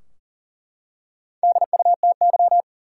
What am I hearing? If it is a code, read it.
Morse code, heard as a keyed tone, about 40 wpm, 700 Hz. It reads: DUTY
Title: Four Letter Words - 40wpm